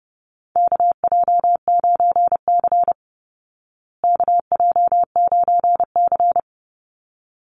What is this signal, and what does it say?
Morse code, heard as a keyed tone, about 30 wpm, 700 Hz. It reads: KJ9C KJ9C
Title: Call Signs Repeated 3x - 30wpm